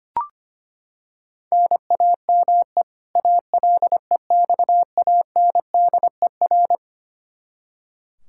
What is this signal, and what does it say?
Morse code, heard as a keyed tone, about 25 wpm, 700 Hz. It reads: NAME ALEXANDER
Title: Practice QSO - Exchange 1 - Beginning - 25wpm